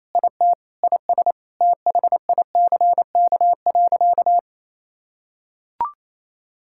Morse code, heard as a keyed tone, about 28 wpm, 700 Hz. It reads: IT IS THICK.
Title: Sentences from Top 1600 Words 28wpm